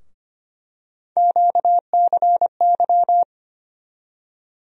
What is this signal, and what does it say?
Morse code, heard as a keyed tone, about 25 wpm, 700 Hz. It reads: QCY
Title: USA Call Sign Postfixes - 3 Letters 25wpm